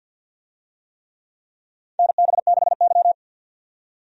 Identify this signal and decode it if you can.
Morse code, heard as a keyed tone, about 47 wpm, 700 Hz. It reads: NB6Y
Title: USA Call Signs - 4 Characters - 50wpm